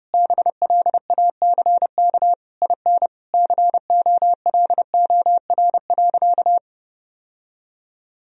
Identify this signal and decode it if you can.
Morse code, heard as a keyed tone, about 30 wpm, 700 Hz. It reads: BLACK IN COLOR.
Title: Sentences from Top 1000 Words - Review Part 2 - 30wpm